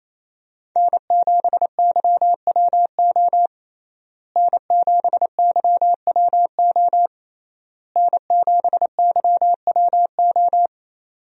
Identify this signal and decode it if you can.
Morse code, heard as a keyed tone, about 28 wpm, 700 Hz. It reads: N7YWO N7YWO N7YWO
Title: USA Call Signs - 5 Characters - Repeated 3x - 28wpm